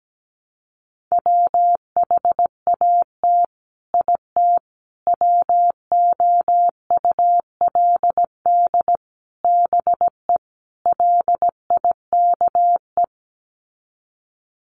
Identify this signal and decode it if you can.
Morse code, heard as a keyed tone, about 17 wpm, 700 Hz. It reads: WHAT IT WOULD BE LIKE
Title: Sets of 5 Words Encore 17wpm